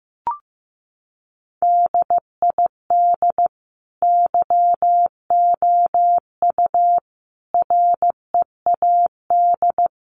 Morse code, heard as a keyed tone, about 15 wpm, 700 Hz. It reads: DID YOU READ
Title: Sentences from Top 1200 Words 15wpm